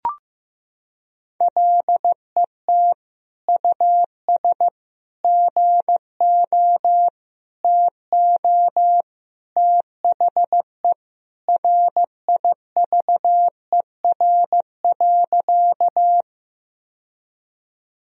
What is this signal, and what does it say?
Morse code, heard as a keyed tone, about 15 wpm, 700 Hz. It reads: LET US GO TO THE RIVER.
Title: Sentences from Top 1000 Words - Review Part 1 - 15wpm